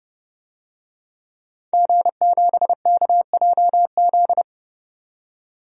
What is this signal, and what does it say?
Morse code, heard as a keyed tone, about 31 wpm, 700 Hz. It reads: G7KJZ
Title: DXCC-100 Call Signs - 5 Characters - 30wpm